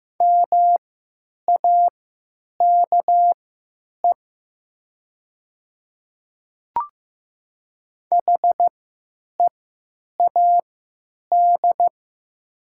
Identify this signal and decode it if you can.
Morse code, heard as a keyed tone, about 15 wpm, spaced as for 9 wpm, 700 Hz. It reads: MAKE HEAD
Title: 3x Character Spacing - Top 500 Words - 15wpm